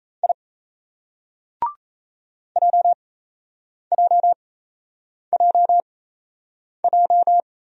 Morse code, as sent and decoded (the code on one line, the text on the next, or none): .. / .--- .--- .--- .---
I JJJJ